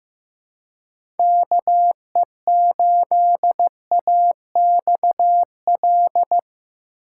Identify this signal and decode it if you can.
Morse code, heard as a keyed tone, about 15 wpm, 700 Hz. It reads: KE8AXL
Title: USA Call Signs - 6 Characters - 2x Character Spacing - 15wpm